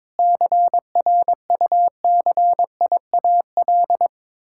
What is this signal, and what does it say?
Morse code, heard as a keyed tone, about 22 wpm, 700 Hz. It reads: CRUCIAL
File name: Sentences with 9-Word Phrases - 22wpm